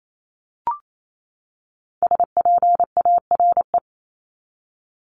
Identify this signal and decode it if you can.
Morse code, heard as a keyed tone, about 28 wpm, 700 Hz. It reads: SPARE
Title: Top 1751-2000 Words 28wpm